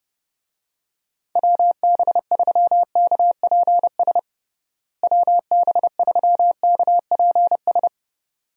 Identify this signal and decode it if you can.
Morse code, heard as a keyed tone, about 29 wpm, 700 Hz. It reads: WB3KPS WB3KPS
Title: USA Call Signs - 6 Characters - Repeated 2x - 30wpm